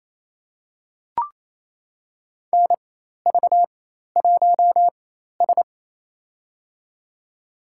Morse code, heard as a keyed tone, about 28 wpm, spaced as for 13 wpm, 700 Hz. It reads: NV1S
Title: USA Call Signs - 4 Characters - 4x Character Spacing - 28wpm